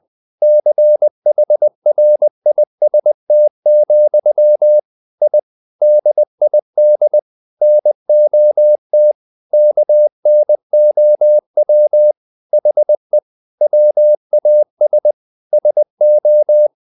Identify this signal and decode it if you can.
Morse code, heard as a keyed tone, about 20 wpm, 600 Hz. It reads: CHRIST, I DID NOT KNOW HE WAS SO
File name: the old man and the sea / the old man and the sea part 1 20wpm